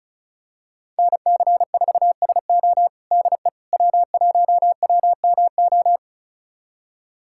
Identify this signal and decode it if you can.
Morse code, heard as a keyed tone, about 36 wpm, 700 Hz. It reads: NC4SO DE W1WMO
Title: USA Callsign DE Callsign - 5 Characters - 35wpm